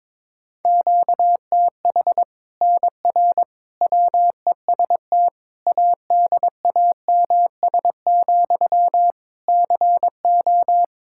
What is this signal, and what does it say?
Morse code, heard as a keyed tone, about 22 wpm, 700 Hz. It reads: QTH NR WEST ADAMS, CO
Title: Practice QSO - Exchange 1 - Beginning - 22wpm